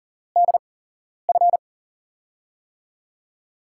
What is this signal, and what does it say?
Morse code, heard as a keyed tone, about 38 wpm, spaced as for 11 wpm, 700 Hz. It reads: DF
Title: USA Call Signs - 5 Characters - 8x Character Spacing - 40wpm